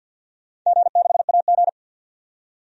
Morse code, heard as a keyed tone, about 52 wpm, 700 Hz. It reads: G6AG
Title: DXCC-100 Call Signs - 4 Characters - 50wpm